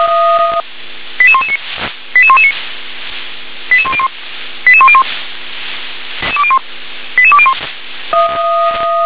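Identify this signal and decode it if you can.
Five-tone selective call (ZVEI), heard as quick runs of five tones, tones of about 70 ms. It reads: 88119 88119 88101 88101 88201
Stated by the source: Five-tone ZVEI call 88176, received by SDR.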